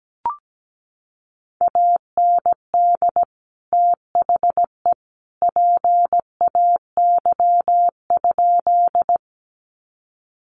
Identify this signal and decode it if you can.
Morse code, heard as a keyed tone, about 17 wpm, 700 Hz. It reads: AND THE PAY?
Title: Sentences from Top 700 Words 17wpm